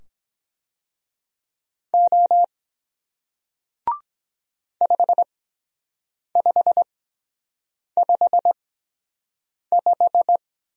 Morse code, heard as a keyed tone, about 23 wpm, spaced as for 7 wpm, 700 Hz. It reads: O 5555